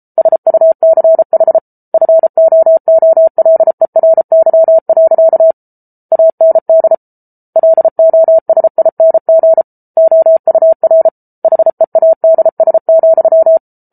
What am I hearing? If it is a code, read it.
Morse code, heard as a keyed tone, about 34 wpm, 650 Hz. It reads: SUCH FOOLERY. AND LOSING OUR HEADS,